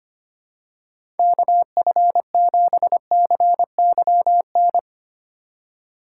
Morse code, heard as a keyed tone, about 25 wpm, 700 Hz. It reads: KF7CYN
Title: USA Call Signs - 6 Characters - 4x Character Spacing - 25wpm